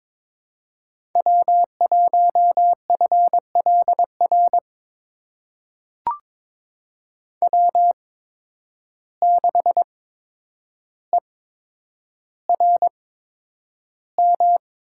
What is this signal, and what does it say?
Morse code, heard as a keyed tone, about 21 wpm, spaced as for 9 wpm, 700 Hz. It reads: W1FLR W6ERM
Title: USA Call Signs - 5 Characters - 8x Character Spacing - 22wpm